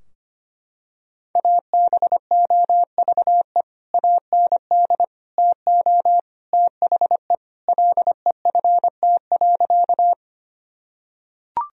Morse code, heard as a keyed tone, about 25 wpm, 700 Hz. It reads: ABOVE AND TO THE LEFT.
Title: Sentences from Top 1000 Words - Review Part 2 - 25wpm